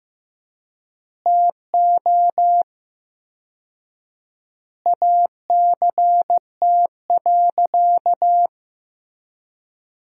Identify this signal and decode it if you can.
Morse code, heard as a keyed tone, about 16 wpm, 700 Hz. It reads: TO ACT.
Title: 4x Word-Spacing - Sentences from Top 200 Words - 15wpm